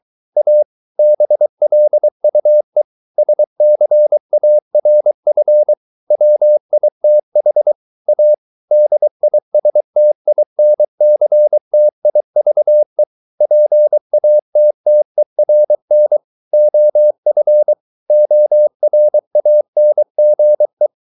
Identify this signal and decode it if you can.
Morse code, heard as a keyed tone, about 23 wpm, 600 Hz. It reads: A BLUE SCARF WITH A DISTINCTIVE PATTERN OF ORANGE